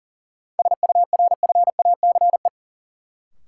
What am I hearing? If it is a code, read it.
Morse code, heard as a keyed tone, about 40 wpm, 700 Hz. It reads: SURFACE
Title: Sets of 4 Words - New Word Drill - 40wpm